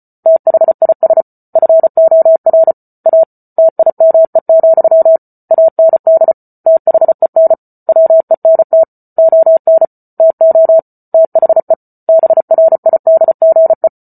Morse code, heard as a keyed tone, about 34 wpm, 650 Hz. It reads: THIS FOR A TIME, AND THEN WENT ON TO THE BRIDGE
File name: Book Two / 08-Chapter 8 - 34 WPM